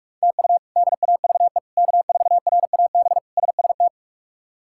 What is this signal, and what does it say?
Morse code, heard as a keyed tone, about 46 wpm, 700 Hz. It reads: TU DAVE K4RAB SST
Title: K1USN SST QSO Practice - 45wpm